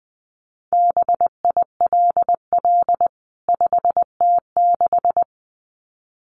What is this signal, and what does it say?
Morse code, heard as a keyed tone, about 20 wpm, 700 Hz. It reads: BILL 5T6
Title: CWT Exchange 20wpm